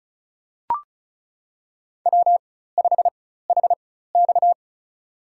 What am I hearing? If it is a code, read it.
Morse code, heard as a keyed tone, about 34 wpm, spaced as for 16 wpm, 700 Hz. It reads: W5HX
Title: USA Call Signs - 4 Characters - 4x Character Spacing - 35wpm